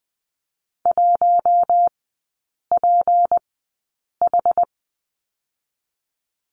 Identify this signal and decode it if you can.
Morse code, heard as a keyed tone, about 20 wpm, spaced as for 8 wpm, 700 Hz. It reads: 1PH
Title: 166 Lesson 27e - Number 7 - Three Characters 20wpm